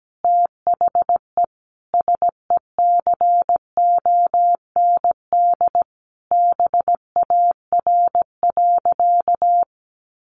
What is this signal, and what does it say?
Morse code, heard as a keyed tone, about 17 wpm, 700 Hz. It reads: THE SECOND BAR.
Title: Easy Phrases 17wpm